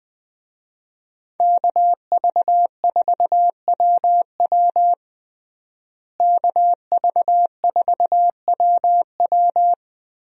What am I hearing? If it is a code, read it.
Morse code, heard as a keyed tone, about 20 wpm, 700 Hz. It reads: KV4WW KV4WW
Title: USA Call Signs - 5 Characters - Repeated 2x - 20wpm